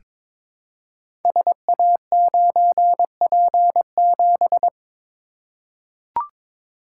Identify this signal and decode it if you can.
Morse code, heard as a keyed tone, about 22 wpm, 700 Hz. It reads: SA9P7
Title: Code Groups - 5 Letter:Numbers - 22wpm